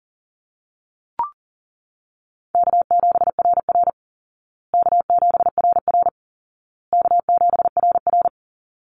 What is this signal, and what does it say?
Morse code, heard as a keyed tone, about 40 wpm, 700 Hz. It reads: K7RR K7RR K7RR
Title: Call Signs Repeated 2x - 40wpm